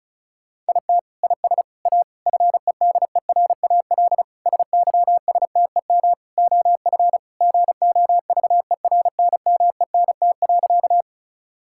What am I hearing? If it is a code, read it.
Morse code, heard as a keyed tone, about 36 wpm, 700 Hz. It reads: IT IS A FEDERAL SYSTEM OF GOVERNMENT.
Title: Sentences from Top 1000 Words - Review Part 5 - 35wpm